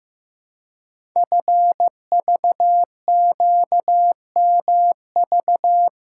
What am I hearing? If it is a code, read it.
Morse code, heard as a keyed tone, about 15 wpm, 700 Hz. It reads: FVQMV
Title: Code Groups - 5 Letter:Numbers - 15wpm